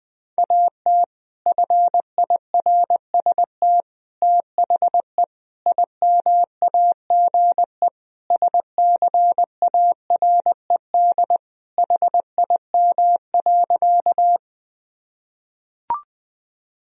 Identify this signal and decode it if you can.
Morse code, heard as a keyed tone, about 20 wpm, 700 Hz. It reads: AT FIRST THE IMAGE SCARED HIM.
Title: Sentences from Top 1000 Words - Review Part 4 - 20wpm